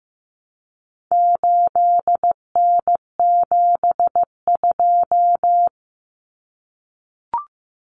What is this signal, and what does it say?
Morse code, heard as a keyed tone, about 15 wpm, 700 Hz. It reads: 8N72
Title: Common Cut Numbers - 4-Digit 15wpm